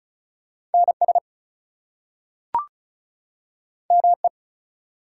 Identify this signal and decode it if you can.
Morse code, heard as a keyed tone, about 36 wpm, 700 Hz. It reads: NS ME
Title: US States and Canadian Provinces 35wpm